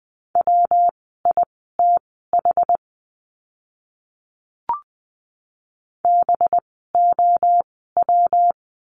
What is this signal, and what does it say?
Morse code, heard as a keyed tone, about 20 wpm, spaced as for 14 wpm, 700 Hz. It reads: WITH BOW
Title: 185 Lesson 30f - Letter b - Words 20wpm